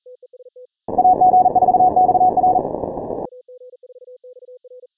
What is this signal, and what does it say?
Morse code, heard as a keyed tone, about 40 wpm, 740 Hz. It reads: AG4XR
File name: final / W1UJ-168-140-MorseRunner